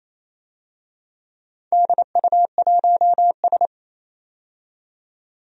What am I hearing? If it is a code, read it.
Morse code, heard as a keyed tone, about 28 wpm, 700 Hz. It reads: DU1S
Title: DXCC-100 Call Signs - 4 Characters - 28wpm